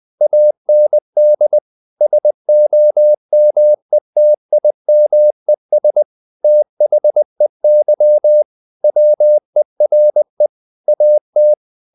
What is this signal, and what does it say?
Morse code, heard as a keyed tone, about 20 wpm, 600 Hz. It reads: AND SOMETIMES THEY WERE AT
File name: the old man and the sea / the old man and the sea part 1 20wpm